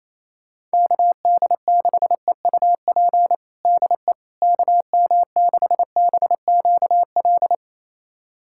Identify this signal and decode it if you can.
Morse code, heard as a keyed tone, about 28 wpm, 700 Hz. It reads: KD6EUP DE KM6BQL
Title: USA Callsign DE Callsign - 6 Characters - 28wpm